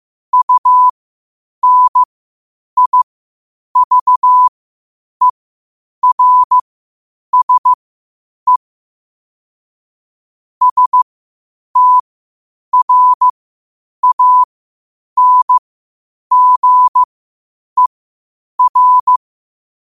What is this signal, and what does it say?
Morse code, heard as a keyed tone, about 15 wpm, spaced as for 8 wpm, 1000 Hz. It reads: UNIVERSE STRANGER